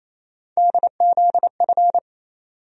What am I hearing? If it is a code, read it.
Morse code, heard as a keyed tone, about 28 wpm, 700 Hz. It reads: DZF
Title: USA Call Sign Postfixes - 3 Letters 28wpm